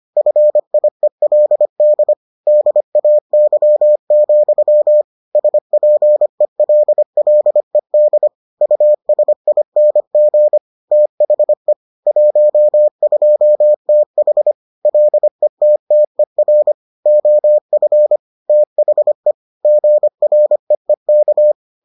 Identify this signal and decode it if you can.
Morse code, heard as a keyed tone, about 25 wpm, 600 Hz. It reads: FIELD DAY, SPELLED USING THE 12TH LETTER OF THE GREEK